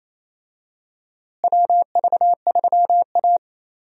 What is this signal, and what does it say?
Morse code, heard as a keyed tone, about 28 wpm, 700 Hz. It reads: WV3A
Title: USA Call Signs - 4 Characters - Repeated 3x - 28wpm